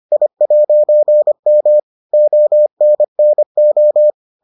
Morse code, heard as a keyed tone, about 25 wpm, 600 Hz. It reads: I'M ONNO